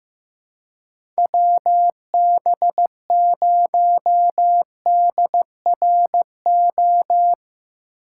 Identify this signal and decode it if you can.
Morse code, heard as a keyed tone, about 15 wpm, 700 Hz. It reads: WB0DRO
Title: USA Call Signs - 6 Characters - 4x Character Spacing - 15wpm